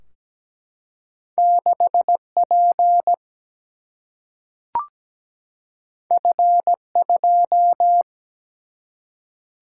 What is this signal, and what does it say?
Morse code, heard as a keyed tone, about 17 wpm, 700 Hz. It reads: 6P F2
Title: Single-Letter Single-Number 17wpm